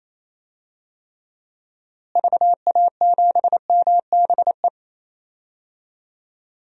Morse code, heard as a keyed tone, about 28 wpm, 700 Hz. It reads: VA7MBE
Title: DXCC-100 Call Signs - 6 Characters - Repeated 2x - 28wpm